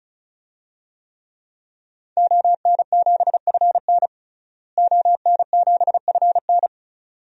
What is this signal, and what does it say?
Morse code, heard as a keyed tone, about 34 wpm, 700 Hz. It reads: ON7FN ON7FN
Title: DXCC-100 Call Signs - 5 Characters - Repeated 2x - 35wpm